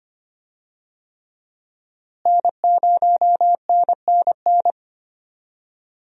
Morse code, heard as a keyed tone, about 25 wpm, 700 Hz. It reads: N0NNN
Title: USA Call Signs - 5 Characters - Repeated 3x - 25wpm